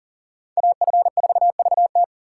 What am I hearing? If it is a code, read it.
Morse code, heard as a keyed tone, about 51 wpm, 700 Hz. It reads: AF4VT